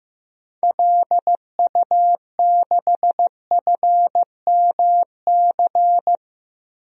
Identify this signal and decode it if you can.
Morse code, heard as a keyed tone, about 16 wpm, 700 Hz. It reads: LU6FMC
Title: DXCC-100 Call Signs - 6 Characters - 15wpm